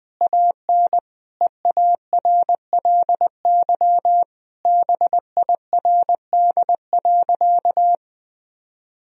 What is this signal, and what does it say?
Morse code, heard as a keyed tone, about 20 wpm, 700 Hz. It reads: AN EARLY BIRD.
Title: Sentences from Top 1000 Words - Review Part 2 - 20wpm